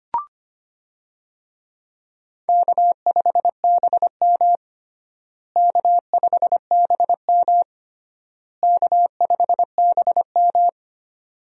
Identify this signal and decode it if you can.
Morse code, heard as a keyed tone, about 25 wpm, 700 Hz. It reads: K5BM K5BM K5BM
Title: USA Call Signs - 4 Characters - Repeated 3x - 25wpm